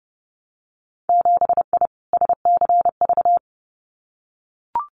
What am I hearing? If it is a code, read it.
Morse code, heard as a keyed tone, about 31 wpm, 700 Hz. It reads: 7I SCV
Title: Winter Field Day Exchanges Encore 30wpm